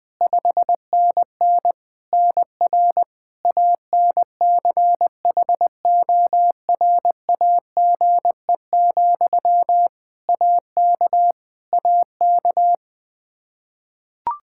Morse code, heard as a keyed tone, about 21 wpm, 700 Hz. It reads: 5NN NR ANCHORAGE, AK AK
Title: Practice QSO - Exchange 1 - Beginning - 20wpm